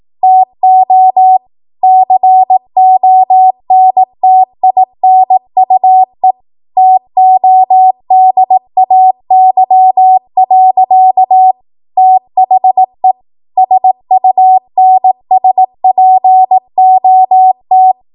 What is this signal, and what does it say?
Morse code, heard as a keyed tone, about 18 wpm, 750 Hz. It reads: TO CONTINUE TODAY. THE SUNSPOT
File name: Morse / 18WPM